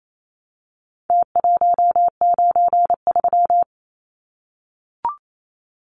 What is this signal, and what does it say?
Morse code, heard as a keyed tone, about 28 wpm, 700 Hz. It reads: T193
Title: Numbers - Contest Style - 4-Digit 28wpm